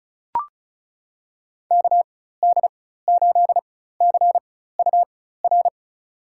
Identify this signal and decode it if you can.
Morse code, heard as a keyed tone, about 36 wpm, spaced as for 16 wpm, 700 Hz. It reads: KD8CUR
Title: USA Call Signs - 6 Characters - 4x Character Spacing - 35wpm